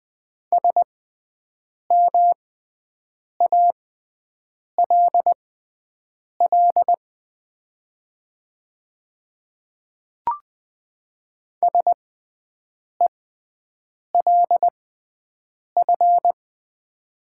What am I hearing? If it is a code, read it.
Morse code, heard as a keyed tone, about 20 wpm, spaced as for 7 wpm, 700 Hz. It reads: SMALL SELF
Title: 6x Character Spacing - Top 500 Words - 20wpm